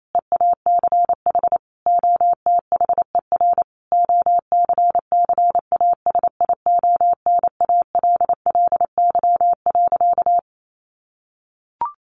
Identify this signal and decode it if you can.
Morse code, heard as a keyed tone, about 28 wpm, 700 Hz. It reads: EACH OTHER OCCASIONALLY.
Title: Sentences from Top 1300 Words 28wpm